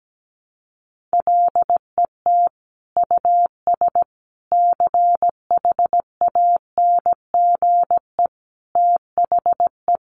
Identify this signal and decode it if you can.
Morse code, heard as a keyed tone, about 17 wpm, 700 Hz. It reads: LET US CHANGE THE
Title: Sentences from Top 800 Words 17wpm